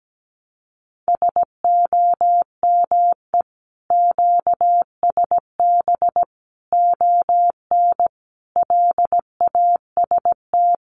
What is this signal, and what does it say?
Morse code, heard as a keyed tone, about 17 wpm, 700 Hz. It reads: SOME QSB ON LAST
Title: Common QSO Phrases 17wpm